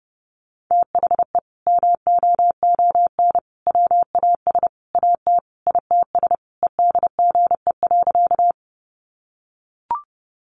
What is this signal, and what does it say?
Morse code, heard as a keyed tone, about 28 wpm, 700 Hz. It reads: THE MOON WAS AT ITS EDGE.
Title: Sentences from Top 1300 Words 30wpm